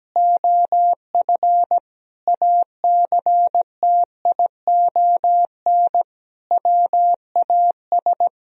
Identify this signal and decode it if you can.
Morse code, heard as a keyed tone, about 17 wpm, 700 Hz. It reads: OF ACTION WAS
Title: Sentences from Top 1000 Words - Review Part 4 - 17wpm